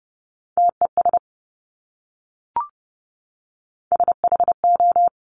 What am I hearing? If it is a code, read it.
Morse code, heard as a keyed tone, about 30 wpm, 690 Hz. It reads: TES SHO
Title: Top Three Letter Combinations in Words 30wpm